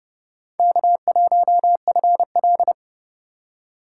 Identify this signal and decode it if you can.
Morse code, heard as a keyed tone, about 30 wpm, 700 Hz. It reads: K1FL
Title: USA Call Signs - 3x Character Spacing - 30wpm